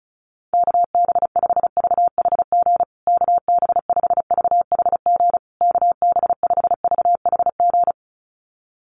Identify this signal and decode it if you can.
Morse code, heard as a keyed tone, about 34 wpm, 700 Hz. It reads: KB5VHG KB5VHG KB5VHG
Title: USA Call Signs Repeated 3x 35wpm